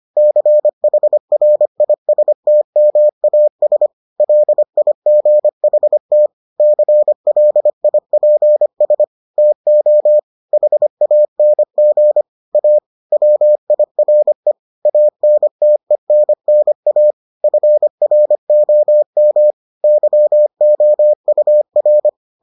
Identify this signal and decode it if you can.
Morse code, heard as a keyed tone, about 25 wpm, 600 Hz. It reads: CHRISTMAS LIGHT CLIPS TO HANG A WIRE ANTENNA FROM YOUR